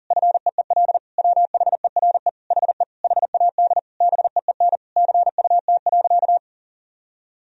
Morse code, heard as a keyed tone, about 40 wpm, 700 Hz. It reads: FEEL WHERE HE HAD BEEN CUT.
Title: Sentences from Top 1000 Words - Review Part 1 - 40wpm